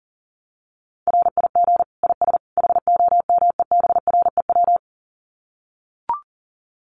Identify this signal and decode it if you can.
Morse code, heard as a keyed tone, about 40 wpm, 700 Hz. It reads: RIG IS HOMEBREW
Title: Common QSO Phrases 40wpm